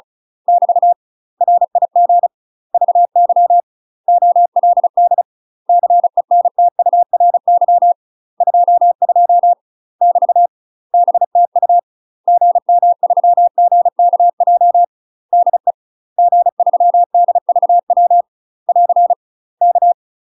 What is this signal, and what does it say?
Morse code, heard as a keyed tone, about 36 wpm, 700 Hz. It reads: <BT> RIG VY OLD CENTURY 22 <BT> BTU GM3GKJ DE G3DVW <AR> K
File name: UK MASTER QSOs TRAINING / 35WPM / UK MASTER QSOs 0018